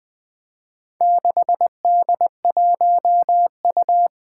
Morse code, heard as a keyed tone, about 20 wpm, 700 Hz. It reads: 6D1U
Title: Four Challenging Letter-Number Combinations 20wpm